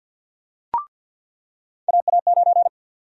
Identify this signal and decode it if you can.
Morse code, heard as a keyed tone, about 53 wpm, 700 Hz. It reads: AA9
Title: USA Call Sign Prefixes - 50wpm